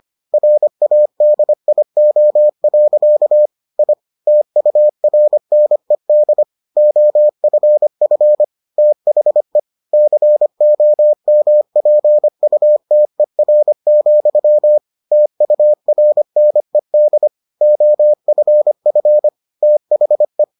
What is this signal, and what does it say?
Morse code, heard as a keyed tone, about 25 wpm, 600 Hz. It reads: RADIO. I TURNED OFF THE COMPUTER, TURNED OFF THE